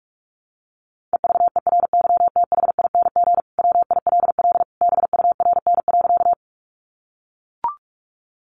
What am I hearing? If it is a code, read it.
Morse code, heard as a keyed tone, about 44 wpm, 700 Hz. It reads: EVERYTHING WILL BURN.
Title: Sentences from Top 1300 Words 45wpm